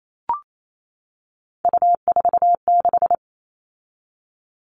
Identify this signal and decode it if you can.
Morse code, heard as a keyed tone, about 28 wpm, 700 Hz. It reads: U46
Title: All Cut Numbers - 3-Digit 28wpm